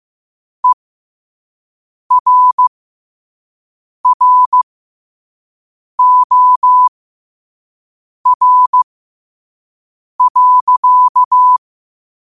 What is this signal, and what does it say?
Morse code, heard as a keyed tone, about 15 wpm, spaced as for 5 wpm, 1000 Hz. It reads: ERROR.
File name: qotdfiles / 5wpm